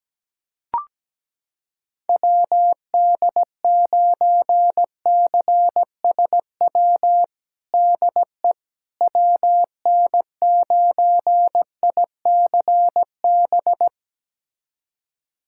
Morse code, heard as a keyed tone, about 17 wpm, 700 Hz. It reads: WD9CSW DE WN9ICB
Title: USA Callsign DE Callsign 17wpm